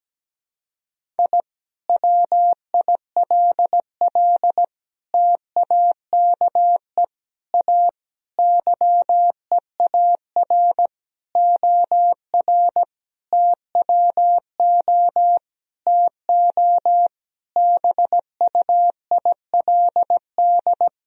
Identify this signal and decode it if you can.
Morse code, heard as a keyed tone, about 17 wpm, 700 Hz. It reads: I WILL TAKE A YEAR OR TWO TO BUILD
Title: Sentences from Top 1000 Words - Review Part 1 - 17wpm